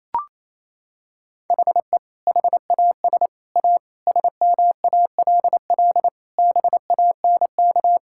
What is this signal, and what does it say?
Morse code, heard as a keyed tone, about 28 wpm, 700 Hz. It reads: HE HAS A SMALL BANK
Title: Sentences from Top 1000 Words - Review Part 6 - 28wpm